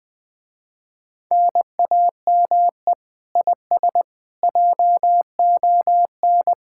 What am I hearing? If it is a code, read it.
Morse code, heard as a keyed tone, about 20 wpm, 700 Hz. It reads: NAME IS JON
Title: Speed-Racing Common QSO Phrases - 17wpm